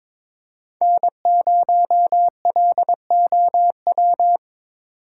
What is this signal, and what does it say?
Morse code, heard as a keyed tone, about 22 wpm, 700 Hz. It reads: N0LOW
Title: USA Call Signs - 5 Characters - 2x Character Spacing - 22wpm